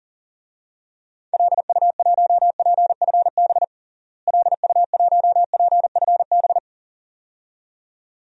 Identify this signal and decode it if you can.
Morse code, heard as a keyed tone, about 38 wpm, 700 Hz. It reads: LU1PFB LU1PFB
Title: DXCC-100 Call Signs - 6 Characters - Repeated 2x - 40wpm